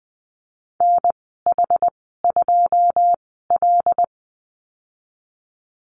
Morse code, heard as a keyed tone, about 20 wpm, spaced as for 14 wpm, 700 Hz. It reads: NH2L
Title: 082 Lesson 14g - Number 2 - Call Signs 20wpm